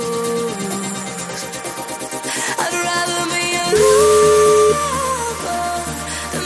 Ringback tone (Europe, 425 Hz), with noise.